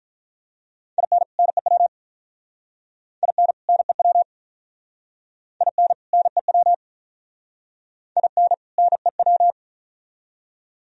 Morse code, heard as a keyed tone, about 44 wpm, 700 Hz. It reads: IN NEW IN NEW IN NEW IN NEW
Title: Speed-Racing - Sets of 2 Words 35wpm